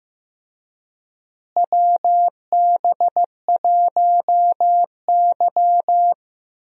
Morse code, heard as a keyed tone, about 15 wpm, 700 Hz. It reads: WB1Y
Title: USA Call Signs - 4 Characters - Repeated 2x - 15wpm